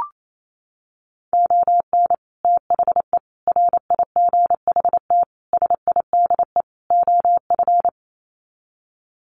Morse code, heard as a keyed tone, about 28 wpm, 700 Hz. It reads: ON THE RIGHT SIDE OF
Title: Sets of 5 Words Encore 28wpm